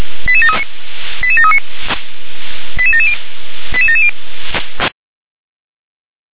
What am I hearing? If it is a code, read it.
Five-tone selective call (ZVEI), heard as quick runs of five tones, tones of about 70 ms. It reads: 88739 88739 88770 88770
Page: /357/SDR/ZVEI/